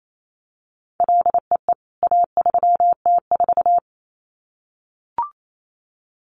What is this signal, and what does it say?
Morse code, heard as a keyed tone, about 28 wpm, 700 Hz. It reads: LEE A3T4
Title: CWT Exchange 28wpm